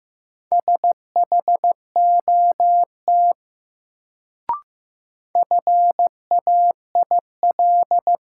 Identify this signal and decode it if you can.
Morse code, heard as a keyed tone, about 15 wpm, 700 Hz. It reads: SHOT FAIL
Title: Sentences from Top 1100 Words - New Word Drill - 15wpm